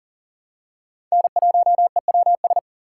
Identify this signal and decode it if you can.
Morse code, heard as a keyed tone, about 40 wpm, 700 Hz. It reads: N1EWS
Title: USA Call Signs - 5 Characters - 2x Character Spacing - 40wpm